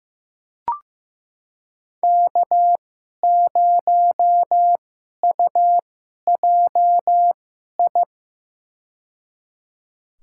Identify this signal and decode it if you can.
Morse code, heard as a keyed tone, about 15 wpm, spaced as for 11 wpm, 700 Hz. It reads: K0UJI
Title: USA Call Signs - 5 Characters - 2x Character Spacing - 15wpm